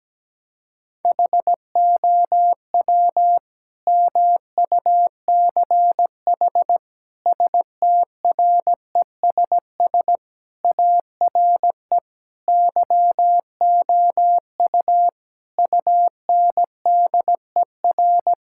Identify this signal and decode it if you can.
Morse code, heard as a keyed tone, about 17 wpm, 700 Hz. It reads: HOW MUCH STRESS ARE YOU UNDER
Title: Sentences from Top 1000 Words - Review Part 6 - 17wpm